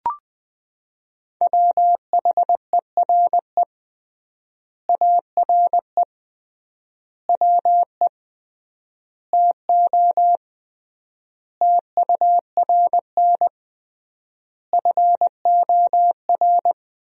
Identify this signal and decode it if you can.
Morse code, heard as a keyed tone, about 20 wpm, 700 Hz. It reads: WHERE ARE WE TO TURN FOR
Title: 3x Word-Spacing - Sentences from Top 200 Words - 20wpm